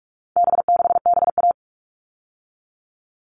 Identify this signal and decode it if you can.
Morse code, heard as a keyed tone, about 44 wpm, 700 Hz. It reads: B6BA
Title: All Cut Numbers - 4-Digit 45wpm